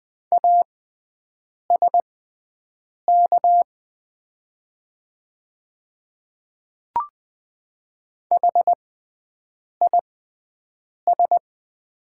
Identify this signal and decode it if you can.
Morse code, heard as a keyed tone, about 20 wpm, spaced as for 7 wpm, 690 Hz. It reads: ASK HIS
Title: 6x Character Spacing - Top 300 Words - 20wpm